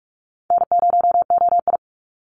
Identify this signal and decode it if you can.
Morse code, heard as a keyed tone, about 44 wpm, 700 Hz. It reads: N0OI
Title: NA SOTA Activator Callsigns 45wpm